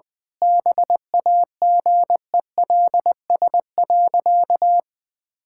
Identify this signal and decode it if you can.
Morse code, heard as a keyed tone, about 20 wpm, 700 Hz. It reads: BAGELS.
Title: All-Age Appropriate Jokes - 20wpm